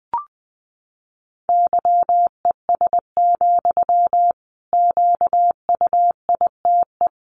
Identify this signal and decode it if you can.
Morse code, heard as a keyed tone, about 20 wpm, 700 Hz. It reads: YES, QUITE
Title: Sentences from Top 700 Words 20wpm